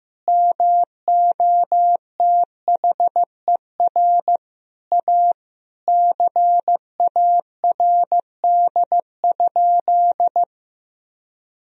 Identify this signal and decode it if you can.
Morse code, heard as a keyed tone, about 15 wpm, 700 Hz. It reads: MOTHER A CARD?
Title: Sentences from Top 1000 Words - Review Part 4 - 15wpm